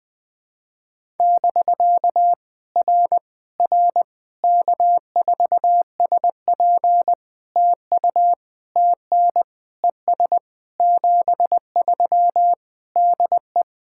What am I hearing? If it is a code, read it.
Morse code, heard as a keyed tone, about 20 wpm, 700 Hz. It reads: <BK> R R K4SP TU TN ES 73 DE
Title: POTA QSO Practice - 20wpm